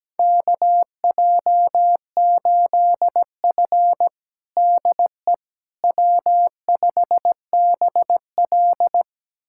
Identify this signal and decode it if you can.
Morse code, heard as a keyed tone, about 17 wpm, 700 Hz. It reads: KJ8F DE W5BL
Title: USA Callsign DE Callsign - 4 Characters - 17wpm